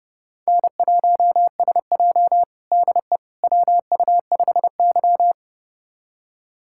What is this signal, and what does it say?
Morse code, heard as a keyed tone, about 30 wpm, 700 Hz. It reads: N1SJ DE WU5Y
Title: USA Callsign DE Callsign - 4 Characters - 30wpm